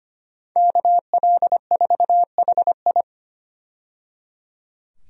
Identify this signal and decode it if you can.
Morse code, heard as a keyed tone, about 25 wpm, 700 Hz. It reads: KL4HI
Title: USA Call Signs - 5 Characters - 25wpm